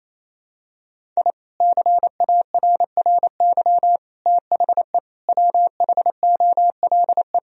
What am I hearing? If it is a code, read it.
Morse code, heard as a keyed tone, about 28 wpm, 700 Hz. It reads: I CARRY THE WHOLE
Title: Sentences from Top 1000 Words - Review Part 1 - 28wpm